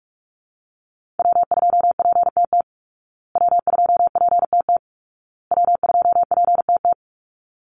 Morse code, heard as a keyed tone, about 46 wpm, 700 Hz. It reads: W2PTT W2PTT W2PTT
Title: Call Signs Repeated 2x - 45wpm